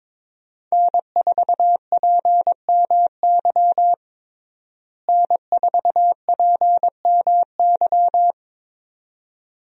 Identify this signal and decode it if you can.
Morse code, heard as a keyed tone, about 22 wpm, 700 Hz. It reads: N4PMY N4PMY
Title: USA Call Signs - 5 Characters - Repeated 2x - 22wpm